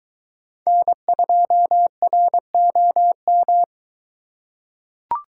Morse code, heard as a keyed tone, about 22 wpm, 700 Hz. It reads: N2ROM